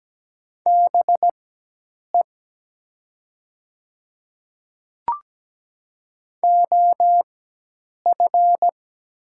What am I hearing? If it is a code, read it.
Morse code, heard as a keyed tone, about 17 wpm, spaced as for 8 wpm, 700 Hz. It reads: BE OF
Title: 4x Character Spacing - Top 100 Words - 17wpm